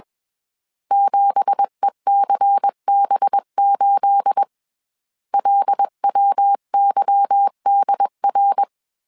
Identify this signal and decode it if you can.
Morse code, heard as a keyed tone, about 21 wpm, 800 Hz. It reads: 7ECB8 LWYDR
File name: CW / DET / 1 9 1500C S C 20wpm 2310